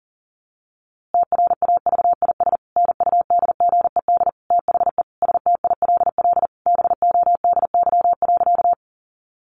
Morse code, heard as a keyed tone, about 40 wpm, 700 Hz. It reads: TRAVIS NUDGED THE STILL BODY.
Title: A Sound Of Thunder 40wpm